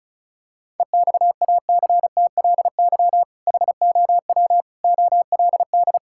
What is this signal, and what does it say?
Morse code, heard as a keyed tone, about 34 wpm, 700 Hz. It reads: EXACTLY HOW OLD
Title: Sentences from Top 1600 Words 35wpm